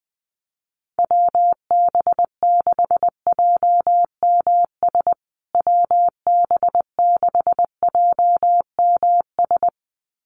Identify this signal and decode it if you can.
Morse code, heard as a keyed tone, about 20 wpm, 700 Hz. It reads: WB6JMS WB6JMS
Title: USA Call Signs Repeated 2x 20wpm